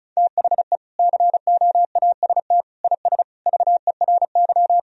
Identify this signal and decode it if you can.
Morse code, heard as a keyed tone, about 36 wpm, 700 Hz. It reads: THE COAST IS VERY
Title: Sentences from Top 1000 Words - Review Part 4 - 35wpm